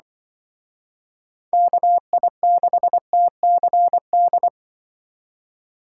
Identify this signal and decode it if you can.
Morse code, heard as a keyed tone, about 25 wpm, 700 Hz. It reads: KI6TCD
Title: Speed Racing - USA Call Signs 20wpm